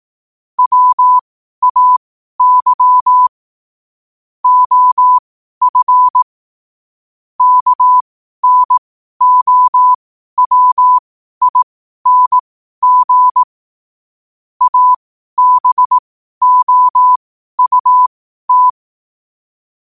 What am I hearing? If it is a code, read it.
Morse code, heard as a keyed tone, about 18 wpm, spaced as for 12 wpm, 1000 Hz. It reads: WAY OF KNOWING ABOUT
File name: qotdfiles / 12wpm